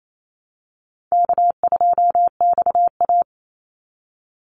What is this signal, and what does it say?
Morse code, heard as a keyed tone, about 28 wpm, 700 Hz. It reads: K2XA
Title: Call Signs - 28wpm